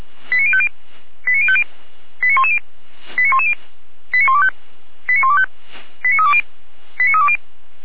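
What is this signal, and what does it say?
Five-tone selective call (ZVEI), heard as quick runs of five tones, tones of about 70 ms. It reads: 89050 89050 89110 89110 89125 89125 89320 89320
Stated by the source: ZVEI five-tone selective call, code 89229, received by SDR.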